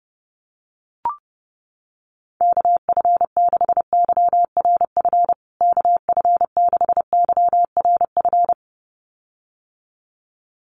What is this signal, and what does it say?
Morse code, heard as a keyed tone, about 30 wpm, 700 Hz. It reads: KF6YRF KF6YRF
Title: USA Call Signs Repeated 2x 30wpm